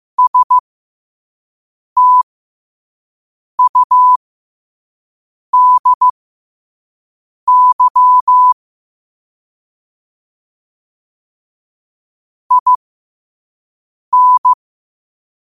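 Morse code, heard as a keyed tone, about 15 wpm, spaced as for 5 wpm, 1000 Hz. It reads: STUDY IN
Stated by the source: Morse code quotes sent at 5 words per minute.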